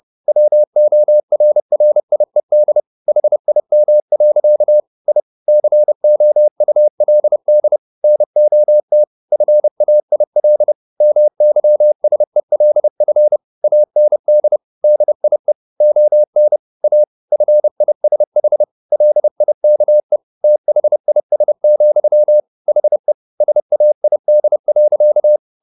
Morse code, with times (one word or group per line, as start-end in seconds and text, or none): 0.3-2.8 WORRIED
3.1-4.8 HIM.
5.1-5.2 I
5.5-7.8 COULD
8.0-9.0 NOT
9.3-10.7 FAIL
11.0-13.4 MYSELF
13.6-14.6 AND
14.8-15.5 DIE
15.8-16.6 ON
16.8-17.0 A
17.3-18.6 FISH
18.9-20.2 LIKE
20.4-22.4 THIS,
22.7-23.1 HE
23.4-25.4 SAID.